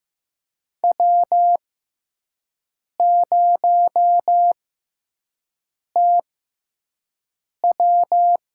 Morse code, heard as a keyed tone, about 15 wpm, spaced as for 5 wpm, 700 Hz. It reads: W0TW
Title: USA Call Signs - 6x Character Spacing - 15wpm